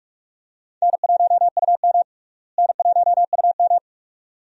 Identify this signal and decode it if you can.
Morse code, heard as a keyed tone, about 46 wpm, 700 Hz. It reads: N1UM N1UM
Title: USA Call Signs - 4 Characters - Repeated 2x - 45wpm